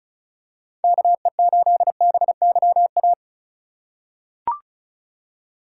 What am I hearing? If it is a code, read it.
Morse code, heard as a keyed tone, about 34 wpm, 700 Hz. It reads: KE8BYA
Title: USA Call Signs - 6 Characters - 2x Character Spacing - 35wpm